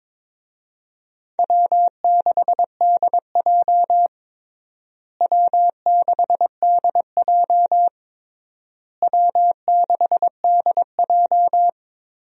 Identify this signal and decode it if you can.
Morse code, heard as a keyed tone, about 22 wpm, 700 Hz. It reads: W6DJ W6DJ W6DJ
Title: USA Call Signs - 4 Characters - Repeated 3x - 22wpm